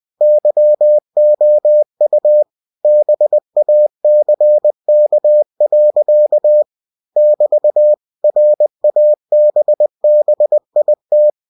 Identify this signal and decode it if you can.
Morse code, heard as a keyed tone, about 20 wpm, 600 Hz. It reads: YOU BACK. <BT> RABBIT
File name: cw-books / winnie-the-pooh / winnie-20wpm-page-0018